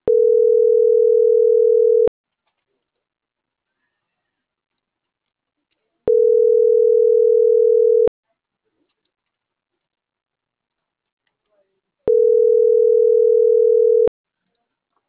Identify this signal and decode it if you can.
Ringback tone (North America), 440 + 480 Hz.
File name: CENTRAL MAERSK / out adrian.moreno